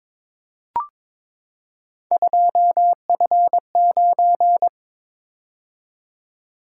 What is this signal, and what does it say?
Morse code, heard as a keyed tone, about 22 wpm, 700 Hz. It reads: 2F9
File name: practice / 22wpm / Three Letter-Number Combinations - 22wpm